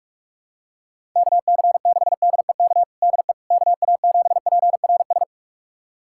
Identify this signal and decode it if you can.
Morse code, heard as a keyed tone, about 42 wpm, 700 Hz. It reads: KC6DEK DE KA7PRS
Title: USA Callsign DE Callsign - 6 Characters - 45wpm